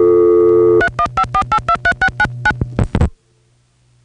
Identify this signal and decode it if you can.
DTMF, heard as telephone touch tones, tones of about 80 ms. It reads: 3154823699